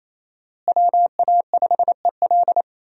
Morse code, heard as a keyed tone, about 28 wpm, 700 Hz. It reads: WA5EL
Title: USA Call Signs - 5 Characters - Repeated 3x - 28wpm